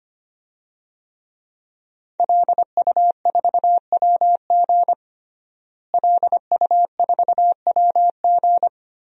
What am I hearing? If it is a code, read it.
Morse code, heard as a keyed tone, about 25 wpm, 700 Hz. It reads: LU4WG LU4WG
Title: DXCC-100 Call Signs - 5 Characters - Repeated 3x - 25wpm